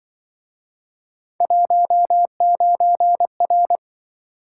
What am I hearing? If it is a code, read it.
Morse code, heard as a keyed tone, about 26 wpm, 700 Hz. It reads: 19R